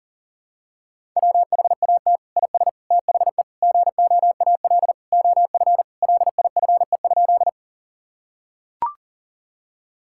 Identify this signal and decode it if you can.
Morse code, heard as a keyed tone, about 40 wpm, 700 Hz. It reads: WHAT IS THE GOAL OF LIFE?
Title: Sentences from Top 1000 Words - Review Part 5 - 40wpm